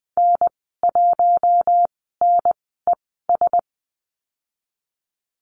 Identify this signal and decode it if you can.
Morse code, heard as a keyed tone, about 20 wpm, spaced as for 14 wpm, 700 Hz. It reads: N1NES
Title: 076 Lesson 13g - Letter L - Call Signs 20wpm